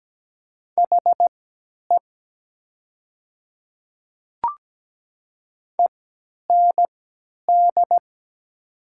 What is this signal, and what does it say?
Morse code, heard as a keyed tone, about 17 wpm, spaced as for 10 wpm, 700 Hz. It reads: HE END
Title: 3x Character Spacing - Top 500 Words - 17wpm